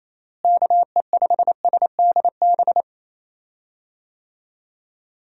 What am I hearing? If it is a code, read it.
Morse code, heard as a keyed tone, about 27 wpm, 700 Hz. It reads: KE5SDB
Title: USA Call Signs - 6 Characters - Repeated 2x - 28wpm